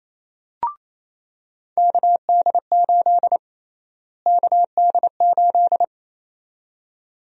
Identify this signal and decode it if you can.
Morse code, heard as a keyed tone, about 27 wpm, 700 Hz. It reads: KD8 KD8
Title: USA Call Sign Prefixes - Repeated 2x - 28wpm